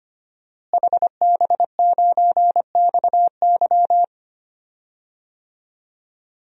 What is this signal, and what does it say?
Morse code, heard as a keyed tone, about 25 wpm, 700 Hz. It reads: HB9XY
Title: DXCC-100 Call Signs - 5 Characters - Repeated 3x - 25wpm